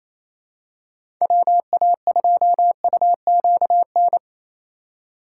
Morse code, heard as a keyed tone, about 28 wpm, 700 Hz. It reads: WA2UQN
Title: USA Call Signs - 6 Characters - 3x Character Spacing - 28wpm